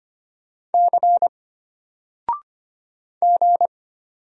Morse code, heard as a keyed tone, about 25 wpm, 700 Hz. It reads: C G
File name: Single-Letter-Number-25wpm